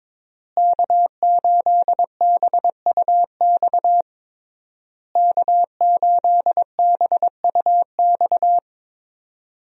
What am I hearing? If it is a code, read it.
Morse code, heard as a keyed tone, about 22 wpm, 700 Hz. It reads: K8BUX K8BUX
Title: USA Call Signs - 5 Characters - Repeated 3x - 22wpm